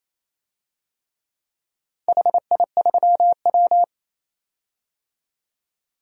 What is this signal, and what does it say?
Morse code, heard as a keyed tone, about 28 wpm, 700 Hz. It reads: HI3W
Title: DXCC-100 Call Signs - 4 Characters - 28wpm